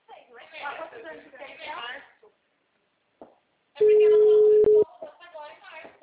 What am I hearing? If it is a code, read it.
Ringback tone (Europe, 425 Hz), with speech.